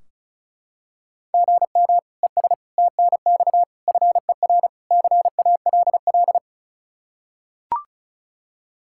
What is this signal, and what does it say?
Morse code, heard as a keyed tone, about 34 wpm, 700 Hz. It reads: GM ES TNX FER CALL
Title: K1USN SST QSO Practice 35wpm